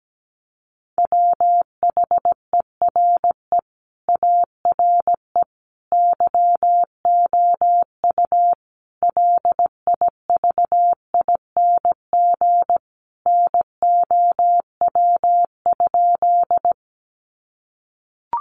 Morse code, heard as a keyed tone, about 17 wpm, 700 Hz. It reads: WHERE ARE YOU LIVING NOW?
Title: Sentences from Top 1100 Words 17wpm